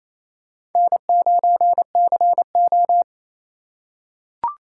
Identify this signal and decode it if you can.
Morse code, heard as a keyed tone, about 28 wpm, 700 Hz. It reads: N9CO
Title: USA Call Signs - 4 Characters - 2x Character Spacing - 28wpm